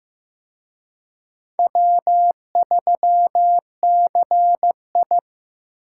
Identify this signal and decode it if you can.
Morse code, heard as a keyed tone, about 18 wpm, 700 Hz. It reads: W3CI
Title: Speed Racing - USA 1x2 Call Signs 15wpm